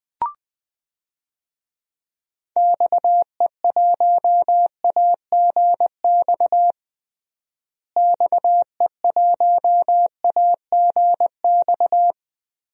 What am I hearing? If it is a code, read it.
Morse code, heard as a keyed tone, about 20 wpm, 700 Hz. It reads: XE1AGX XE1AGX
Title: DXCC-100 Call Signs - 6 Characters - Repeated 2x - 20wpm